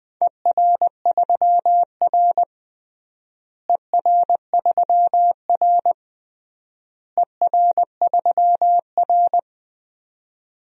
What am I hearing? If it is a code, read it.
Morse code, heard as a keyed tone, about 20 wpm, 700 Hz. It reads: ER3R ER3R ER3R
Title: DXCC-100 Call Signs - 4 Characters - Repeated 3x - 20wpm